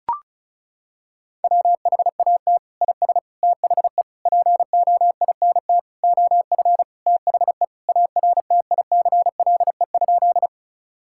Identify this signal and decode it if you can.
Morse code, heard as a keyed tone, about 36 wpm, 700 Hz. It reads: WHAT IS THE POINT OF THE ARTICLE?
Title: Sentences from Top 1000 Words - Review Part 5 - 35wpm